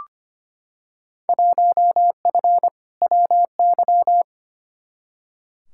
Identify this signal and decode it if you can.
Morse code, heard as a keyed tone, about 25 wpm, 700 Hz. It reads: 1F WY
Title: Field Day Exchanges Encore - 25wpm